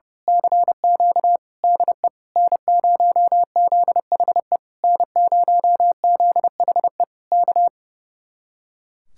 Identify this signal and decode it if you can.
Morse code, heard as a keyed tone, about 28 wpm, 700 Hz. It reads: CQ DE N0ZHE N0ZHE K
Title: Speed-Racing Common QSO Phrases - 30wpm